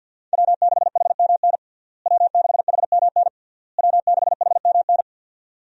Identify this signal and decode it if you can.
Morse code, heard as a keyed tone, about 50 wpm, 700 Hz. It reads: W6HMN W6HMN W6HMN
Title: USA Call Signs - 5 Characters - Repeated 3x - 50wpm